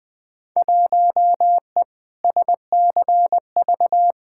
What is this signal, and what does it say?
Morse code, heard as a keyed tone, about 20 wpm, 700 Hz. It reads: 1E SCV
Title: Field Day Exchanges - 20wpm